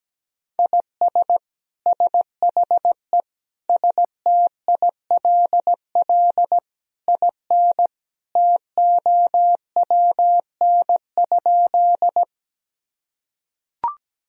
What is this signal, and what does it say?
Morse code, heard as a keyed tone, about 17 wpm, 700 Hz. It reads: IS SHE STILL IN TOWN?
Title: Sentences from Top 1000 Words - Review Part 2 - 17wpm